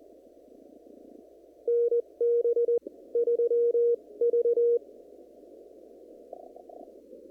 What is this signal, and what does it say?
Morse code, heard as a keyed tone, about 20 wpm, spaced as for 16 wpm, 470 Hz. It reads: NB3V
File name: CW / kt5j wpx cw-op-k5pi